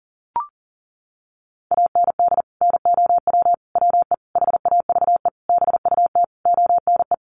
Morse code, heard as a keyed tone, about 37 wpm, 700 Hz. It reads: AND NOW WE HAVE BUT ONE
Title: Sentences from Top 1500 Words 40wpm